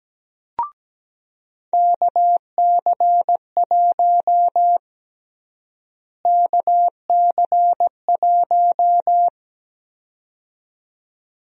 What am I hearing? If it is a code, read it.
Morse code, heard as a keyed tone, about 17 wpm, 700 Hz. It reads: KC1 KC1
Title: USA Call Sign Prefixes - Repeated 2x - 17wpm